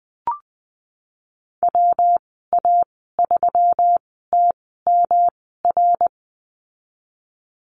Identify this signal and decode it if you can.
Morse code, heard as a keyed tone, about 20 wpm, spaced as for 14 wpm, 700 Hz. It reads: WA3TMR
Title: 154 Lesson 25g - Letter p - Call Signs 20wpm